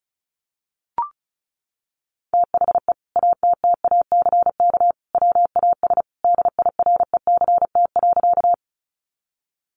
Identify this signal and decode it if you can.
Morse code, heard as a keyed tone, about 34 wpm, 700 Hz. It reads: THE ATTACK WAS DIRECT.
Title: Sentences from Top 1500 Words 35wpm